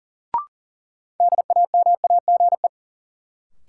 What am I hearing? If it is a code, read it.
Morse code, heard as a keyed tone, about 40 wpm, 700 Hz. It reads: DAMAGE
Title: Sentences from Top 1100 Words - New Word Drill - 40wpm